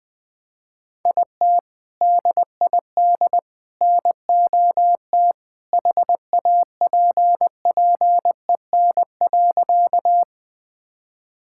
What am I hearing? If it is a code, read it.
Morse code, heard as a keyed tone, about 20 wpm, 700 Hz. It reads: IT DID NOT HAPPEN.
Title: Sentences with 3-Word Phrases - 20wpm